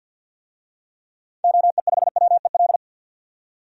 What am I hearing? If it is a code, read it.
Morse code, heard as a keyed tone, about 52 wpm, 700 Hz. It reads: OE5WEL
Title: DXCC-100 Call Signs - 6 Characters - 50wpm